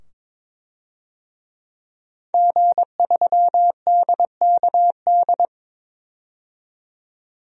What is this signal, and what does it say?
Morse code, heard as a keyed tone, about 22 wpm, 700 Hz. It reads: G3DKD
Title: DXCC-100 Call Signs - 5 Characters - 22wpm